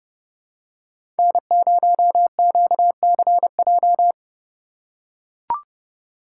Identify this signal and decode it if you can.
Morse code, heard as a keyed tone, about 31 wpm, 700 Hz. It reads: N0QCJ
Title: USA Call Signs - 5 Characters - 3x Character Spacing - 30wpm